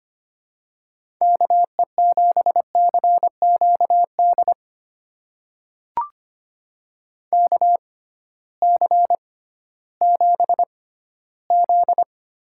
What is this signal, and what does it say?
Morse code, heard as a keyed tone, about 25 wpm, spaced as for 15 wpm, 700 Hz. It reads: KE7CQD KC7Z
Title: USA Call Signs - 6 Characters - 6x Character Spacing - 25wpm